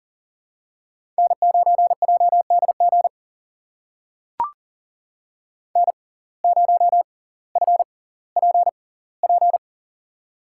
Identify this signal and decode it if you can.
Morse code, heard as a keyed tone, about 40 wpm, spaced as for 20 wpm, 700 Hz. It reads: N9JDG N0FPP
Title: USA Call Signs - 5 Characters - 6x Character Spacing - 40wpm